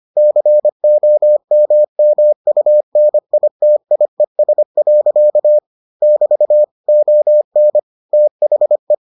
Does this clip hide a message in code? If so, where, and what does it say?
Yes: Morse code from 0.2 to 8.9 s, heard as a keyed tone, about 25 wpm, 600 Hz. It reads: COMMUNITIES. <BT> ON THE